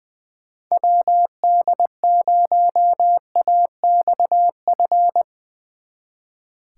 Morse code, heard as a keyed tone, about 22 wpm, 700 Hz. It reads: WD0AXF
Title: Speed Racing - USA Call Signs 20wpm